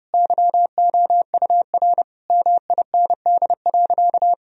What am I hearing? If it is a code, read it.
Morse code, heard as a keyed tone, about 30 wpm, 700 Hz. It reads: YOUR MIND.
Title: Sentences from Top 1000 Words - Review Part 3 - 30wpm